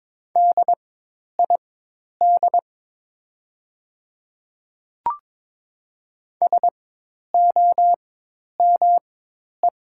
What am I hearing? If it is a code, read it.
Morse code, heard as a keyed tone, about 22 wpm, spaced as for 10 wpm, 700 Hz. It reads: DID SOME
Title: 4x Character Spacing - Top 100 Words - 22wpm